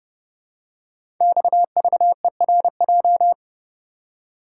Morse code, heard as a keyed tone, about 30 wpm, 700 Hz. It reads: XVERJ
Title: Code Groups - 5 Letters - 30wpm